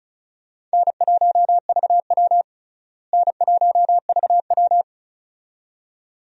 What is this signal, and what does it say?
Morse code, heard as a keyed tone, about 36 wpm, 700 Hz. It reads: N1VW N1VW
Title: USA Call Signs - 4 Characters - Repeated 2x - 35wpm